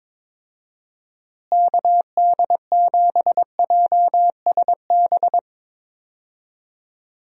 Morse code, heard as a keyed tone, about 22 wpm, 700 Hz. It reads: KD7JSB
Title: USA Call Signs - 6 Characters - Repeated 3x - 22wpm